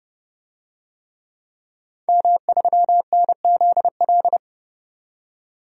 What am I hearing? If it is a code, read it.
Morse code, heard as a keyed tone, about 30 wpm, 700 Hz. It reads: M3NZL
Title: DXCC-100 Call Signs - 5 Characters - 30wpm